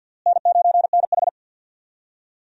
Morse code, heard as a keyed tone, about 52 wpm, 700 Hz. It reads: N9NH
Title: USA Call Signs - 4 Characters - Repeated 3x - 50wpm